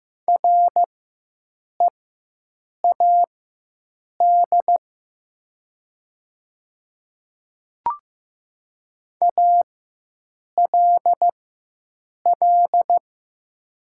Morse code, heard as a keyed tone, about 15 wpm, spaced as for 7 wpm, 700 Hz. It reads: READ ALL
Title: 4x Character Spacing - Top 300 Words - 15wpm